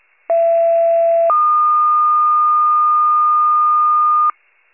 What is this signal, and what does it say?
Two-tone page, heard as a one-second tone, then a three-second tone, about 670 then 1220 Hz. It is heard from 0.3 to 4.3 s.
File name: wcalert